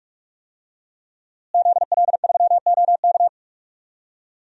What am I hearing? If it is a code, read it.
Morse code, heard as a keyed tone, about 46 wpm, 700 Hz. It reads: ZL3OK
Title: DXCC-100 Call Signs - 5 Characters - Repeated 2x - 45wpm